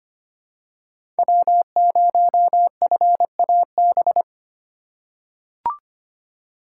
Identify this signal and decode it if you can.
Morse code, heard as a keyed tone, about 25 wpm, 700 Hz. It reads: W0FAB
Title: USA Call Signs - 5 Characters - 4x Character Spacing - 25wpm